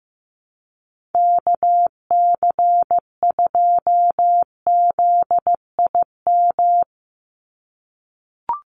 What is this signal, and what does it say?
Morse code, heard as a keyed tone, about 15 wpm, 700 Hz. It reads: KC2ZIM
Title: USA Call Signs 15wpm